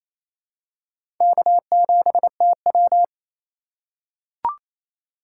Morse code, heard as a keyed tone, about 28 wpm, 700 Hz. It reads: K7TW
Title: USA Call Signs - 4 Characters - 2x Character Spacing - 28wpm